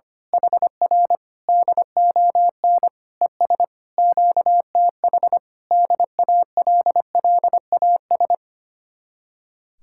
Morse code, heard as a keyed tone, about 24 wpm, 700 Hz. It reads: HR DON ES QTH DALLAS
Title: Speed-Racing Common QSO Phrases - 25wpm